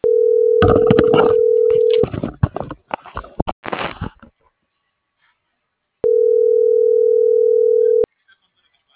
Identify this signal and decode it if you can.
Ringback tone (North America), 440 + 480 Hz.